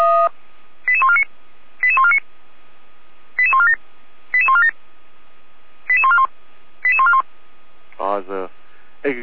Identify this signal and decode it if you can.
Five-tone selective call (ZVEI), heard as quick runs of five tones, tones of about 70 ms, with speech. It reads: 88149 88149 88147 88147 88141 88141